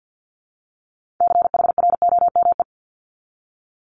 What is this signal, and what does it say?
Morse code, heard as a keyed tone, about 50 wpm, 700 Hz. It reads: CHROME
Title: Multi-letter Phonemes 50wpm